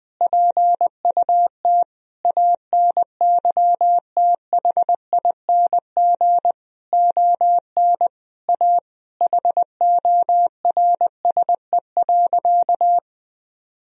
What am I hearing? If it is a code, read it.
Morse code, heard as a keyed tone, about 19 wpm, 700 Hz. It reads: PUT ANYTHING ON A HORSE.
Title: Sentences from Top 1000 Words - Review Part 1 - 20wpm